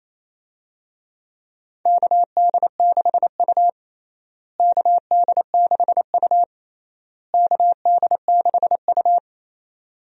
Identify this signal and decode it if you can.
Morse code, heard as a keyed tone, about 28 wpm, 700 Hz. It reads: KD6U KD6U KD6U
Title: USA Call Signs - 4 Characters - Repeated 3x - 28wpm